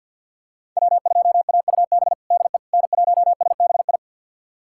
Morse code, heard as a keyed tone, about 48 wpm, 700 Hz. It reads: W2AUB DE N1SBI
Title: USA Callsign DE Callsign 50wpm